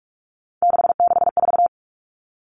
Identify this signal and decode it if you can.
Morse code, heard as a keyed tone, about 44 wpm, 700 Hz. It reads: 664
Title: Common Cut Numbers - 3-Digit 45wpm